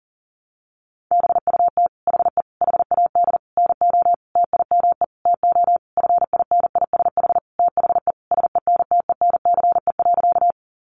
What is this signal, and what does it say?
Morse code, heard as a keyed tone, about 40 wpm, 700 Hz. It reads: BUT HE HAD NO TIME TO FINISH THE SENTENCE.
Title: Sentences from Top 1200 Words 40wpm